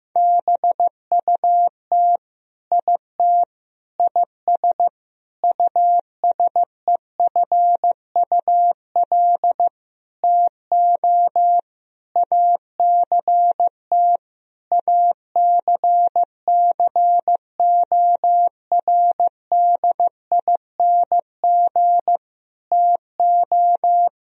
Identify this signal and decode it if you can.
Morse code, heard as a keyed tone, about 15 wpm, 700 Hz. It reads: BUT IT IS USEFUL TO ACT ACCORDING TO
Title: Sentences from Top 1000 Words - Review Part 4 - 15wpm